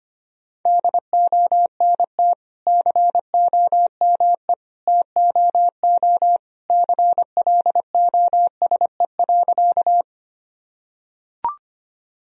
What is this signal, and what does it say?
Morse code, heard as a keyed tone, about 25 wpm, 700 Hz. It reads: DONT COME TOO CLOSE.
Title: Sentences from Top 1000 Words - Review Part 1 - 25wpm